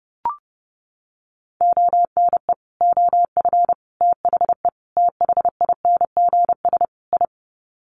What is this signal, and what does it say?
Morse code, heard as a keyed tone, about 30 wpm, 700 Hz. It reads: ONE OF THE THINGS I
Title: Sets of 5 Words 30wpm